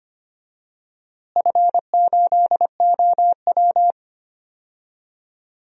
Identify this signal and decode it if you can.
Morse code, heard as a keyed tone, about 25 wpm, 700 Hz. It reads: F8OW
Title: DXCC-100 Call Signs - 4 Characters - Repeated 3x - 25wpm